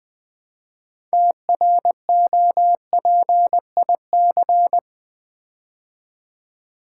Morse code, heard as a keyed tone, about 20 wpm, 700 Hz. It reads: TROPIC
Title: Build A Word - 2 Letters At A Time - 8-Letter Words - 20wpm